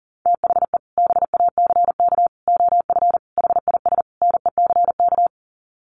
Morse code, heard as a keyed tone, about 40 wpm, 700 Hz. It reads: THE BACK OF HIS NECK
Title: Sets of 5 Words Encore 40wpm